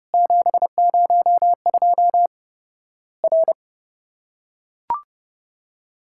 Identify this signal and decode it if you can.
Morse code, heard as a keyed tone, about 30 wpm, 700 Hz. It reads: 702 R
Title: Worked All Europe DX Contest - QTC Exchange - 30wpm